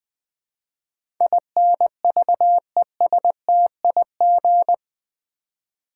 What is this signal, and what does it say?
Morse code, heard as a keyed tone, about 20 wpm, 700 Hz. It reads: INVESTIG
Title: Build A Word - 4 Letters At A Time - 11-Letter Words - 20wpm